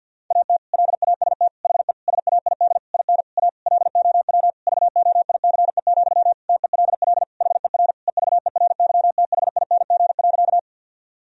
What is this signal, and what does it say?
Morse code, heard as a keyed tone, about 52 wpm, 700 Hz. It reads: AT LAST HE SAID IN A LOW VOICE, TELL HER EVERYTHING.
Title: Sentences from Top 1000 Words - Review Part 1 - 50wpm